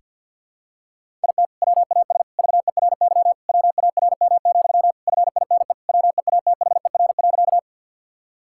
Morse code, heard as a keyed tone, about 52 wpm, 700 Hz. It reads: IT WAS VERY WARM, FINE WEATHER.
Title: Sentences from Top 1000 Words - Review Part 5 - 50wpm